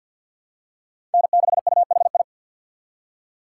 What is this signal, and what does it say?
Morse code, heard as a keyed tone, about 50 wpm, 700 Hz. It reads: N6UHI
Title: USA Call Signs - 5 Characters - 3x Character Spacing - 50wpm